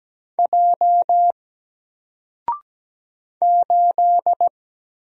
Morse code, heard as a keyed tone, about 17 wpm, 700 Hz. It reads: J 8
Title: Single Character (Letter:Number:Punctuation:Prosign) 17wpm